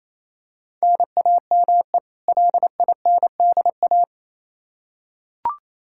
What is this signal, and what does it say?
Morse code, heard as a keyed tone, about 28 wpm, 700 Hz. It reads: NAME LINDA
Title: Practice QSO - Exchange 1 - Beginning - 28wpm